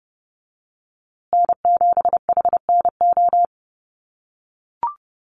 Morse code, heard as a keyed tone, about 30 wpm, 700 Hz. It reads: N7HNO
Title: USA Call Signs Repeated 2x 30wpm